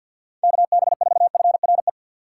Morse code, heard as a keyed tone, about 53 wpm, 700 Hz. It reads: KB4FRE
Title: USA Call Signs - 6 Characters - 8x Character Spacing - 50wpm